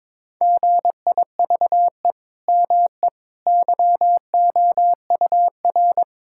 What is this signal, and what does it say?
Morse code, heard as a keyed tone, about 22 wpm, 700 Hz. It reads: GIVE ME YOUR
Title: Sentences from Top 1000 Words - Review Part 6 - 22wpm